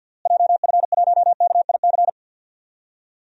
Morse code, heard as a keyed tone, about 51 wpm, 700 Hz. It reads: JF1KIC
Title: DXCC-100 Call Signs - 6 Characters - Repeated 3x - 50wpm